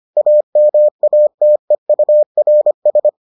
Morse code, heard as a keyed tone, about 25 wpm, 600 Hz. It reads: AMATEURS